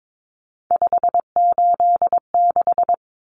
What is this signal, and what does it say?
Morse code, heard as a keyed tone, about 22 wpm, 700 Hz. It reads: 586
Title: Numbers - 3-Digit 22wpm